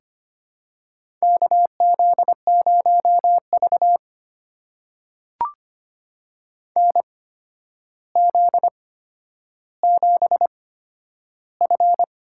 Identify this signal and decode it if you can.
Morse code, heard as a keyed tone, about 25 wpm, spaced as for 11 wpm, 700 Hz. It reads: KZ0V NZ7F
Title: USA Call Signs - 4 Characters - 8x Character Spacing - 25wpm